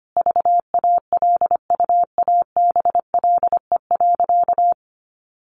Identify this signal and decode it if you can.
Morse code, heard as a keyed tone, about 25 wpm, 700 Hz. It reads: VALUABLE.
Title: Sentences from Top 1100 Words 25wpm